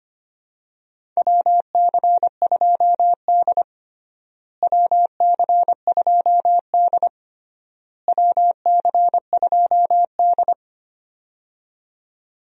Morse code, heard as a keyed tone, about 25 wpm, 700 Hz. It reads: WC2D WC2D WC2D
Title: USA Call Signs - 4 Characters - Repeated 3x - 25wpm